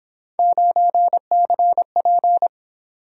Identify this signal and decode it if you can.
Morse code, heard as a keyed tone, about 28 wpm, 700 Hz. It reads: 9CP